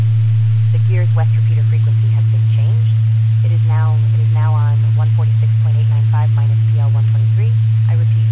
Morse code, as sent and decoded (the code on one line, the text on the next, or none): none